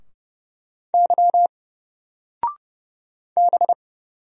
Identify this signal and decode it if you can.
Morse code, heard as a keyed tone, about 30 wpm, 700 Hz. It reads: Y B
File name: Single-Letter-Number-30wpm